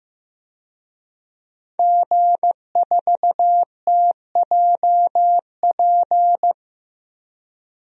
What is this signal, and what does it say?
Morse code, heard as a keyed tone, about 15 wpm, 700 Hz. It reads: G4TJP
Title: DXCC-100 Call Signs - 5 Characters - 15wpm